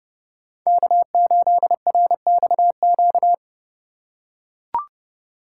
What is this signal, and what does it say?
Morse code, heard as a keyed tone, about 30 wpm, 700 Hz. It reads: K8RXQ